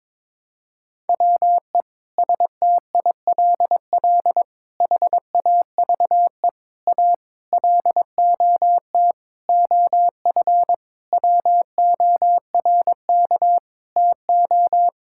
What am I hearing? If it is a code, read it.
Morse code, heard as a keyed tone, about 22 wpm, 700 Hz. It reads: WE STILL HAVE A LOT OF WORK TO
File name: Sentences with 9-Word Phrases - 22wpm